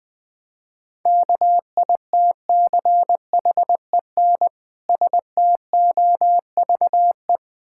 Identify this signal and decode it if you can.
Morse code, heard as a keyed tone, about 20 wpm, 700 Hz. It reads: KITCHEN STOVE
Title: Difficult Words - Speed-Racing with Hints - 20wpm